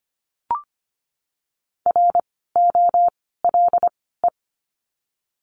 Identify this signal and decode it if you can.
Morse code, heard as a keyed tone, about 25 wpm, spaced as for 16 wpm, 700 Hz. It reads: ROLE
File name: course / 25:10wpm / 094 Lesson 16f - Letter c - Words 25wpm